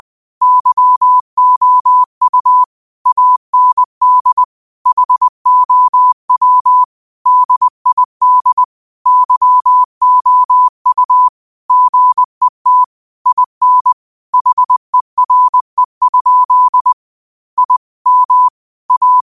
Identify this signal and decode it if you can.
Morse code, heard as a keyed tone, about 20 wpm, 1000 Hz. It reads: YOU AND HOW DID YOU GET IN HERE? I M A
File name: qotdfiles / 20wpm